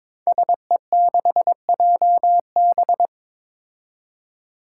Morse code, heard as a keyed tone, about 22 wpm, 700 Hz. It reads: SE6JB
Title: Five Challenging Letter-Number Combinations 22wpm